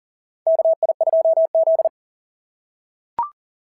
Morse code, heard as a keyed tone, about 40 wpm, 660 Hz. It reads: KI2Z
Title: SOTA QSO Practice 40wpm